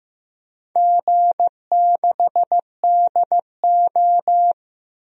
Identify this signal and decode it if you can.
Morse code, heard as a keyed tone, about 15 wpm, 700 Hz. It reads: G6DO
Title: DXCC-100 Call Signs - 4 Characters - Repeated 3x - 15wpm